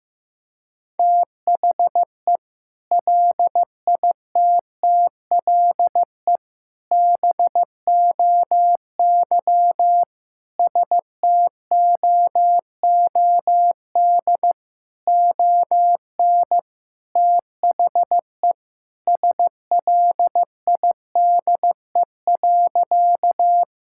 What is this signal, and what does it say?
Morse code, heard as a keyed tone, about 15 wpm, 700 Hz. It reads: THE LITTLE BOY STOOD ON THE SLIDE.
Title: Sentences from Top 1600 Words 15wpm